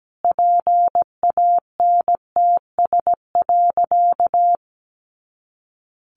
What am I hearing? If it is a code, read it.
Morse code, heard as a keyed tone, about 17 wpm, 700 Hz. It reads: PANTS.
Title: A Sound Of Thunder 17wpm